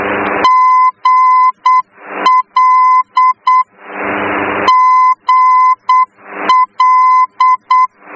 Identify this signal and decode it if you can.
Morse code, heard as a keyed tone, about 8 wpm, 1020 Hz. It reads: GL GL
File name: germany / mx / GL-D